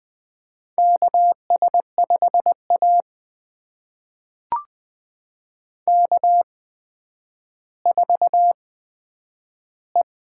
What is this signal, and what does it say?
Morse code, heard as a keyed tone, about 20 wpm, spaced as for 10 wpm, 700 Hz. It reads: KS5A K4E
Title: USA Call Signs - 4 Characters - 8x Character Spacing - 20wpm